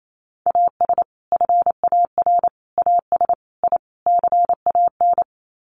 Morse code, heard as a keyed tone, about 28 wpm, 700 Hz. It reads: AS FAR AS I CAN
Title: Sets of 5 Words Encore 28wpm